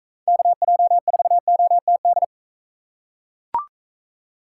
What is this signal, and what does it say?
Morse code, heard as a keyed tone, about 41 wpm, 700 Hz. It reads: KJ4OTD